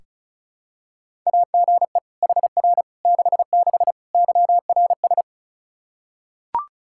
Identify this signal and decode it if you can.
Morse code, heard as a keyed tone, about 36 wpm, 700 Hz. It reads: AGE HR 66 YRS
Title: Practice QSO - Exchange 3 - More Info - 35wpm